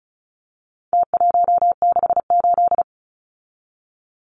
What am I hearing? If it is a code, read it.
Morse code, heard as a keyed tone, about 34 wpm, 700 Hz. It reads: T168
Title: Numbers - Contest Style - 4-Digit 35wpm